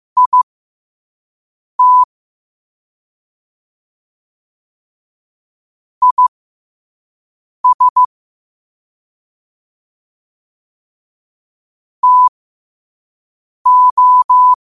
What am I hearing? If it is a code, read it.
Morse code, heard as a keyed tone, about 15 wpm, spaced as for 5 wpm, 1000 Hz. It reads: IT IS TO